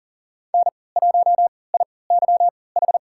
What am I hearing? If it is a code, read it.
Morse code, heard as a keyed tone, about 37 wpm, spaced as for 22 wpm, 700 Hz. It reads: N1IYH
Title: USA Call Signs - 3x Character Spacing - 40wpm